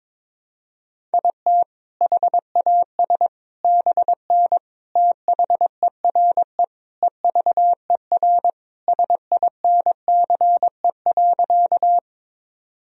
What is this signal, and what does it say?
Morse code, heard as a keyed tone, about 22 wpm, 700 Hz. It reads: IT HAS BN THERE EVER SINCE.
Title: Sentences from Top 400 Words and Abbreviations 22wpm